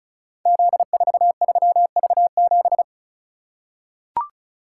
Morse code, heard as a keyed tone, about 36 wpm, 700 Hz. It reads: Z43V7
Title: Code Groups - 5 Letter:Numbers - 35wpm